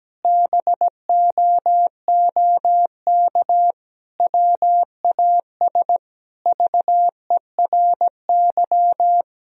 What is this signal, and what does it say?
Morse code, heard as a keyed tone, about 17 wpm, 700 Hz. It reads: BOOK WAS VERY
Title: Sentences from Top 1000 Words - Review Part 4 - 17wpm